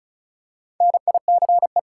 Morse code, heard as a keyed tone, about 36 wpm, 700 Hz. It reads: NICE
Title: Four Letter Words - 35wpm